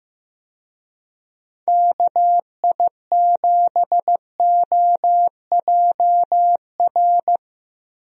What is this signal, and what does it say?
Morse code, heard as a keyed tone, about 16 wpm, 700 Hz. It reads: KI7OJR
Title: USA Call Signs - 6 Characters - Repeated 2x - 15wpm